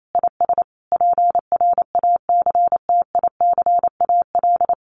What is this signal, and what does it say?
Morse code, heard as a keyed tone, about 28 wpm, 700 Hz. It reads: IS PRACTICAL
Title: Sentences from Top 1400 Words 28wpm